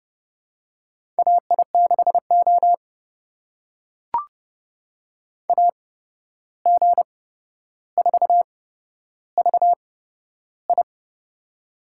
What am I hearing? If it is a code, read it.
Morse code, heard as a keyed tone, about 30 wpm, spaced as for 12 wpm, 700 Hz. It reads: AI6O AG4VI
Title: USA Call Signs - 8x Character Spacing - 30wpm